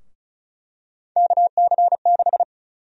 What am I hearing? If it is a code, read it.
Morse code, heard as a keyed tone, about 36 wpm, 700 Hz. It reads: KC6
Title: USA Call Sign Prefixes - 35wpm